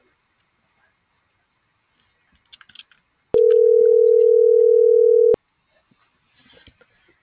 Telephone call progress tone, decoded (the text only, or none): ringback tone